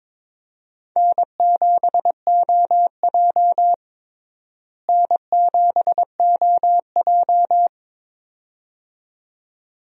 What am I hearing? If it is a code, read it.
Morse code, heard as a keyed tone, about 22 wpm, 700 Hz. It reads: N7OJ N7OJ
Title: USA Call Signs - 4 Characters - Repeated 2x - 22wpm